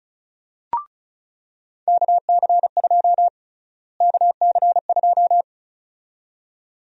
Morse code, heard as a keyed tone, about 36 wpm, 700 Hz. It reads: KC2 KC2
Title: USA Call Sign Prefixes - Repeated 2x - 35wpm